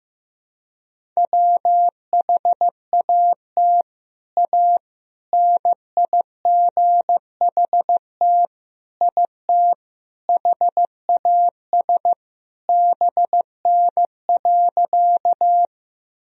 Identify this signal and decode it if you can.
Morse code, heard as a keyed tone, about 15 wpm, 700 Hz. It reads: WHAT A NIGHT IT HAS BN.
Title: Sentences from Top 300 Words and Abbreviations 15wpm